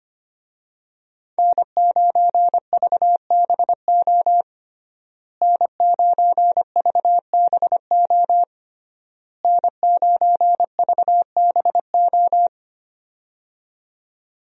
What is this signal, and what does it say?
Morse code, heard as a keyed tone, about 25 wpm, 700 Hz. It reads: N9VBO N9VBO N9VBO
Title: USA Call Signs - 5 Characters - Repeated 3x - 25wpm